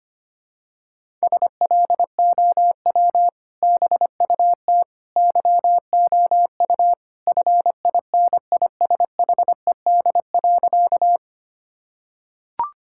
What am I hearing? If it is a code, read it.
Morse code, heard as a keyed tone, about 25 wpm, 700 Hz. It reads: SLOW BUT YOU FINISHED.
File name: practice / 25wpm / Sentences with 2-Word Phrases - 25wpm